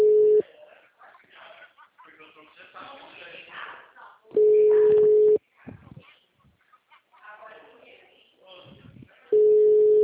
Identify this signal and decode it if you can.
Ringback tone (Europe, 425 Hz), with speech.
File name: out- -2010- - - .54787